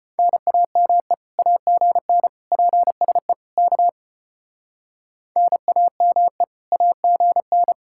Morse code, heard as a keyed tone, about 32 wpm, 700 Hz. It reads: NAME AGN PSE K NAME AGN